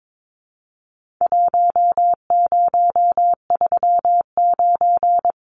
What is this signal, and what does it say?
Morse code, heard as a keyed tone, about 22 wpm, 700 Hz. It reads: 1039
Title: Numbers - Contest Style - 4-Digit 22wpm